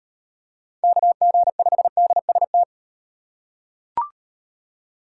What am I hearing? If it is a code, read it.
Morse code, heard as a keyed tone, about 38 wpm, 700 Hz. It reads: KG5DST